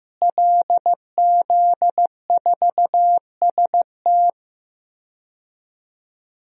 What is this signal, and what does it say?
Morse code, heard as a keyed tone, about 15 wpm, 700 Hz. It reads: LZ4ST
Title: DXCC-100 Call Signs - 5 Characters - Repeated 2x - 15wpm